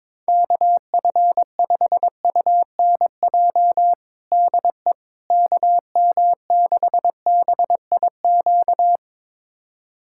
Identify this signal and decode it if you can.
Morse code, heard as a keyed tone, about 22 wpm, 700 Hz. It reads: KF5UNJ DE KM6BIQ
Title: USA Callsign DE Callsign 22wpm